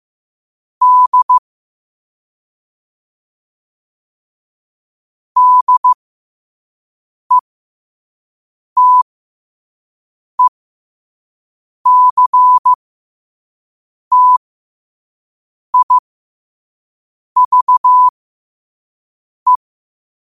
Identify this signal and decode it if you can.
Morse code, heard as a keyed tone, about 15 wpm, spaced as for 5 wpm, 1000 Hz. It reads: D DETECTIVE